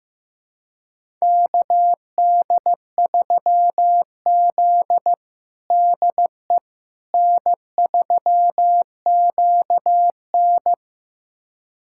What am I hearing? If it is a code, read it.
Morse code, heard as a keyed tone, about 15 wpm, 700 Hz. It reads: KD3Z DE N3QN
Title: USA Callsign DE Callsign - 4 Characters - 15wpm